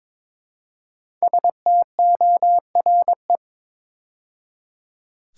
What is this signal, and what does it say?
Morse code, heard as a keyed tone, about 22 wpm, 700 Hz. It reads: STORE
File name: 5 Letter Words - First 3 Letters Given - 22wpm